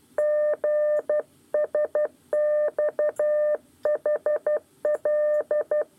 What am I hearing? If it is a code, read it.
Morse code, heard as a keyed tone, about 12 wpm, 600 Hz. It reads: GSXHL
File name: 8 lezione cw 1